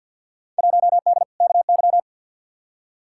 Morse code, heard as a keyed tone, about 53 wpm, 700 Hz. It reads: 1D KY
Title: Field Day Exchanges - 50wpm